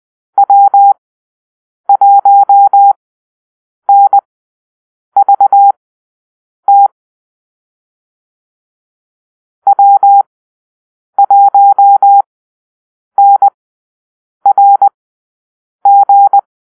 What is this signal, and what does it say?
Morse code, heard as a keyed tone, about 20 wpm, spaced as for 7 wpm, 800 Hz. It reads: W1NVT W1NRG
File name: arrl / Call-Sign Drill-10